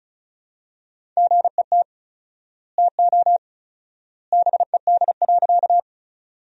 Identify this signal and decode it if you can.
Morse code, heard as a keyed tone, about 34 wpm, 700 Hz. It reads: GET TO BED.
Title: 4x Word-Spacing - Sentences from Top 500 Words - 35wpm